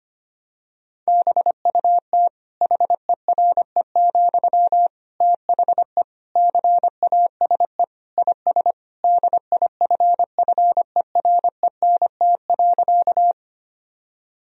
Morse code, heard as a keyed tone, about 25 wpm, 700 Hz. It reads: BUT HERE, THE CASE IS DIFFERENT.
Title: Sentences from Top 1000 Words - Review Part 3 - 25wpm